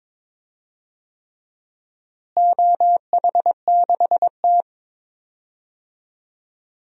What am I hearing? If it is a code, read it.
Morse code, heard as a keyed tone, about 22 wpm, 700 Hz. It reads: OH6T
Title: DXCC-100 Call Signs - 4 Characters - Repeated 3x - 22wpm